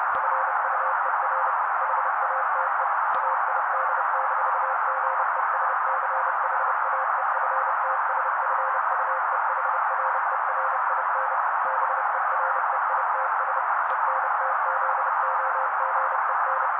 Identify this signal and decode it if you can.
Morse code, heard as a keyed tone, about 30 wpm, 530 Hz. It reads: FUER HATE AID-GESCHAEFTSFUEHRERIN BEREITS ENTZOGEN
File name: daily / 29wpm noise